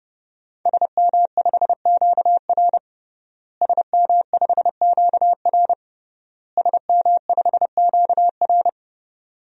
Morse code, heard as a keyed tone, about 30 wpm, 700 Hz. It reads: SM5QR SM5QR SM5QR
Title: DXCC-100 Call Signs - 5 Characters - Repeated 3x - 30wpm